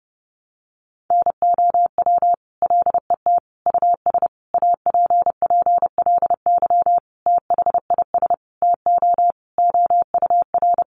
Morse code, heard as a keyed tone, about 30 wpm, 700 Hz. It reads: NOW LET US APPLY THIS TO OUR
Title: Sentences from Top 1200 Words 30wpm